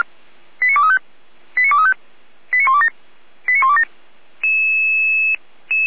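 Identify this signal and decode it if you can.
Five-tone selective call (ZVEI), heard as quick runs of five tones, tones of about 70 ms. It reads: 89235 89235 89127 89127
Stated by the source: ZVEI five-tone selective call received on SDR.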